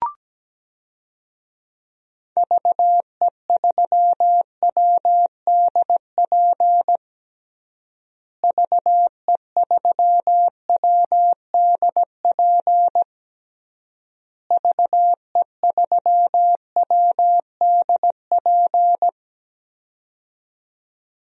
Morse code, heard as a keyed tone, about 17 wpm, 700 Hz. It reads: VE3WDP VE3WDP VE3WDP
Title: DXCC-100 Call Signs - 6 Characters - Repeated 3x - 17wpm